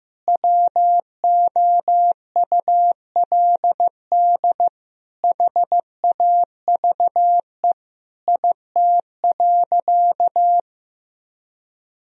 Morse code, heard as a keyed tone, about 15 wpm, 700 Hz. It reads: WOULD HAVE IT.
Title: Sentences from Top 1000 Words - Review Part 3 - 15wpm